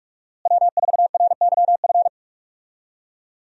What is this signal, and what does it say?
Morse code, heard as a keyed tone, about 46 wpm, 700 Hz. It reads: W4RYF
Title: USA Call Signs - 5 Characters - Repeated 2x - 45wpm